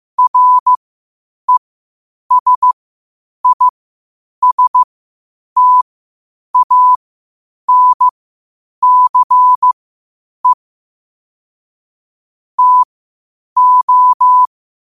Morse code, heard as a keyed tone, about 15 wpm, spaced as for 8 wpm, 1000 Hz. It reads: RESISTANCE TO
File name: qotdfiles / 8wpm